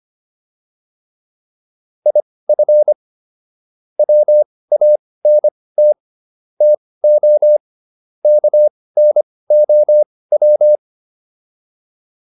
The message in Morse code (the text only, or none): IF WANT TO KNOW